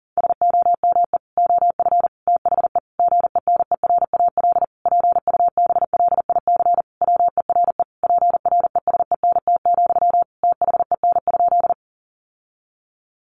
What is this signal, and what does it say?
Morse code, heard as a keyed tone, about 40 wpm, 700 Hz. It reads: SOME OF THE GENERAL PUBLIC WERE PRESENT, THEN?
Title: Sentences from Top 800 Words 40wpm